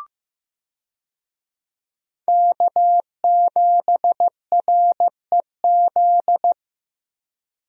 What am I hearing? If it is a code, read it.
Morse code, heard as a keyed tone, about 15 wpm, 700 Hz. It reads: K7REZ
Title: USA Call Signs - 5 Characters - Repeated 2x - 15wpm